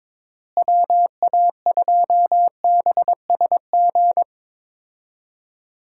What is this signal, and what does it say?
Morse code, heard as a keyed tone, about 22 wpm, 700 Hz. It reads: WA2BSG
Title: USA Call Signs - 6 Characters - Repeated 2x - 22wpm